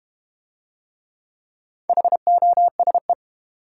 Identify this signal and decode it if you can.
Morse code, heard as a keyed tone, about 30 wpm, 700 Hz. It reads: HOSE
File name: Easy Words - Speed-Racing with Hints - 25wpm